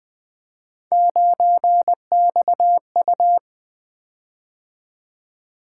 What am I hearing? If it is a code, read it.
Morse code, heard as a keyed tone, about 20 wpm, 700 Hz. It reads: 9XU
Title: Three Letter-Number Combinations - 20wpm